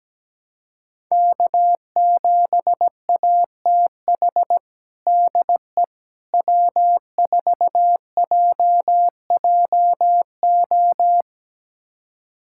Morse code, heard as a keyed tone, about 17 wpm, 700 Hz. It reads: K7ATH DE W4JJO
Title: USA Callsign DE Callsign - 5 Characters - 17wpm